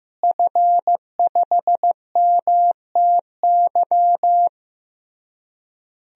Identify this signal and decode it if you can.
Morse code, heard as a keyed tone, about 15 wpm, 700 Hz. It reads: F5MTY
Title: DXCC-100 Call Signs - 5 Characters - Repeated 3x - 15wpm